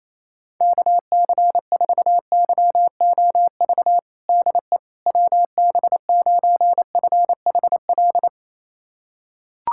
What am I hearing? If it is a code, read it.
Morse code, heard as a keyed tone, about 28 wpm, 700 Hz. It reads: KC4YOV DE WB9FHL
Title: USA Callsign DE Callsign - 6 Characters - 28wpm